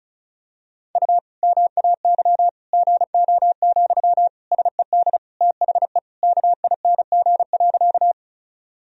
Morse code, heard as a keyed tone, about 34 wpm, 700 Hz. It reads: U MAY GO, SED THE KING.
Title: Sentences from Top 400 Words and Abbreviations 35wpm